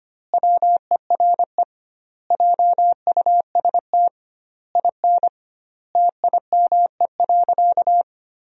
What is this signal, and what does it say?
Morse code, heard as a keyed tone, about 25 wpm, 700 Hz. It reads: WERE JUST IN TIME.
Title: 2x Word-Spacing - Sentences from Top 300 Words - 25wpm